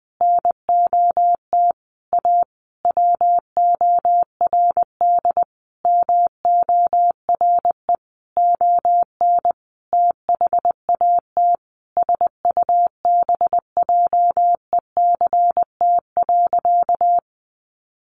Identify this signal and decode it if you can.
Morse code, heard as a keyed tone, about 20 wpm, 700 Hz. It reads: NOT A WORD MORE ON THAT SUBJECT.
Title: Sentences from Top 800 Words 20wpm